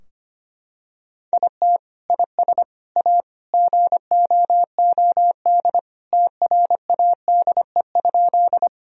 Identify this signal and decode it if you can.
Morse code, heard as a keyed tone, about 25 wpm, 700 Hz. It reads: IT IS A GOOD TRADE?
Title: Sentences from Top 1000 Words - Review Part 4 - 25wpm